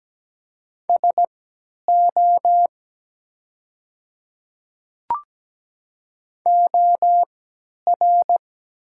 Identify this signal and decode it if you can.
Morse code, heard as a keyed tone, about 17 wpm, spaced as for 10 wpm, 700 Hz. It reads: SO OR
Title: 3x Character Spacing - Top 100 Words - 17wpm